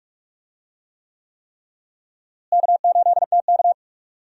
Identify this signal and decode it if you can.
Morse code, heard as a keyed tone, about 43 wpm, 700 Hz. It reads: K8TK
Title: USA Call Signs - 4 Characters - Repeated 3x - 45wpm